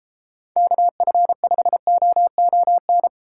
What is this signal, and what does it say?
Morse code, heard as a keyed tone, about 32 wpm, 700 Hz. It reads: KF5OON